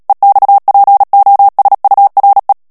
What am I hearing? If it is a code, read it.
Morse code, heard as a keyed tone, about 37 wpm, 800 Hz. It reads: EXPOSURE
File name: morse mp3 / ANGLAIS texte clair 3 vitesse 35 mots minute